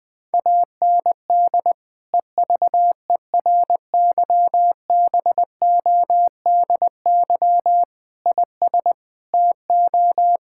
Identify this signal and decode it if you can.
Morse code, heard as a keyed tone, about 20 wpm, 700 Hz. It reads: AND EVERYBODY IS TO
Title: Sentences from Top 1600 Words 20wpm